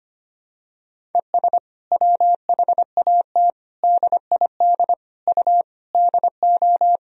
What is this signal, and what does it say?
Morse code, heard as a keyed tone, about 25 wpm, 700 Hz. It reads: ES WHAT DID U DO
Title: Sentences from Top 100 Words and Abbreviations 25wpm